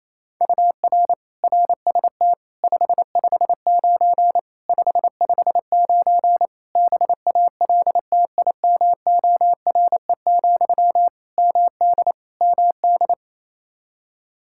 Morse code, heard as a keyed tone, about 28 wpm, 700 Hz. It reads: UR RST 559 559 BALTIMORE, MD MD
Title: Practice QSO - Exchange 1 - Beginning - 28wpm